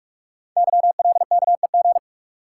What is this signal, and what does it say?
Morse code, heard as a keyed tone, about 44 wpm, 700 Hz. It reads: YLKEG
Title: Code Groups - 5 Letters - 45wpm